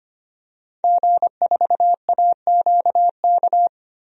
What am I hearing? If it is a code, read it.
Morse code, heard as a keyed tone, about 25 wpm, 700 Hz. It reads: G4AQK
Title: DXCC-100 Call Signs - 5 Characters - 25wpm